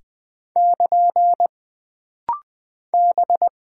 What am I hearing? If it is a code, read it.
Morse code, heard as a keyed tone, about 20 wpm, 700 Hz. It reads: <KN> B
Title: Single Character (Letter:Number:Punctuation:Prosign) - Rapid-Fire - 20wpm